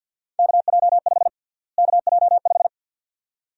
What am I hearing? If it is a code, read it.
Morse code, heard as a keyed tone, about 50 wpm, 700 Hz. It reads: KJ5 KJ5
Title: USA Call Sign Prefixes - Repeated 2x - 50wpm